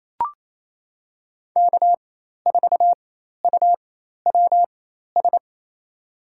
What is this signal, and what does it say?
Morse code, heard as a keyed tone, about 28 wpm, spaced as for 13 wpm, 700 Hz. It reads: K4UWS
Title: USA Call Signs - 5 Characters - 4x Character Spacing - 28wpm